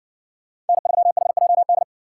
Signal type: Morse code

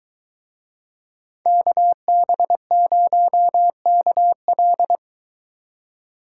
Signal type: Morse code

